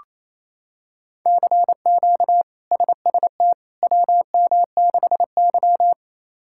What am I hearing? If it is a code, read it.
Morse code, heard as a keyed tone, about 28 wpm, 700 Hz. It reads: CQ SST WM6Y
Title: K1USN SST QSO Practice - 28wpm